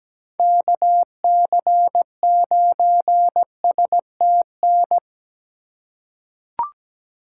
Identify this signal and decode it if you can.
Morse code, heard as a keyed tone, about 17 wpm, 700 Hz. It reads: KC9STN
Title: USA Call Signs - 6 Characters - 3x Character Spacing - 17wpm